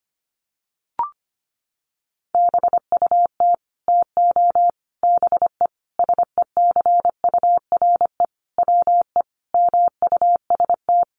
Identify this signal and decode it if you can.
Morse code, heard as a keyed tone, about 25 wpm, 700 Hz. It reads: BUT TO BE SECURE WE MUST
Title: Sentences from Top 1200 Words 25wpm